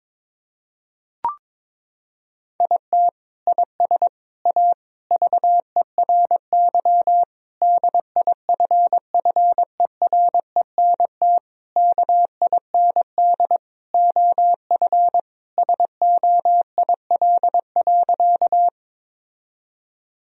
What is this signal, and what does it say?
Morse code, heard as a keyed tone, about 21 wpm, 700 Hz. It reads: IT IS A VERY DIFFERENT KIND OF SOIL.
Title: Sentences from Top 1000 Words - Review Part 4 - 22wpm